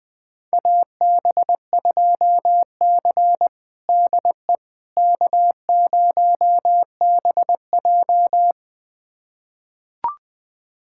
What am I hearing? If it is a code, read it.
Morse code, heard as a keyed tone, about 20 wpm, 700 Hz. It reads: AB2C DE K0BJ
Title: USA Callsign DE Callsign - 4 Characters - 20wpm